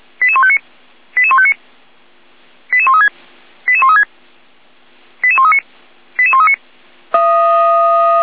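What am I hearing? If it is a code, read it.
Five-tone selective call (ZVEI), heard as quick runs of five tones, tones of about 70 ms. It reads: 88159 88159 88136 88136 88139 88139
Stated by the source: ZVEI five-tone selective call 88147 received by SDR.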